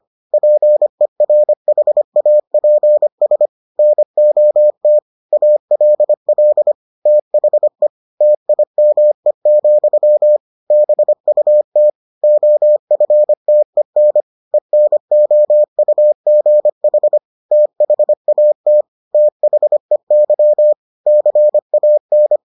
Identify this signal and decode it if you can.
Morse code, heard as a keyed tone, about 25 wpm, 600 Hz. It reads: PERHAPS NOT ALL THE TIME, BUT OFTEN ENOUGH THAT THEY CAN